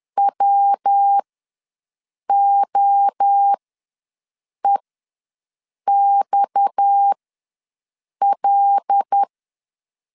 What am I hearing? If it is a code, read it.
Morse code, heard as a keyed tone, about 10 wpm, spaced as for 6 wpm, 800 Hz. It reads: WOEXL
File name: LEZIONI / 6 PKX0 / 6 PKX0 ws 10wpm 4220